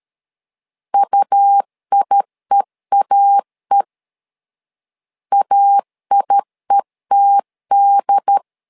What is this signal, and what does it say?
Morse code, heard as a keyed tone, about 13 wpm, 800 Hz. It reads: UIEAE AIETD
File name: LEZIONI / 2 UHDN / 2 UHDN 12wpm 1456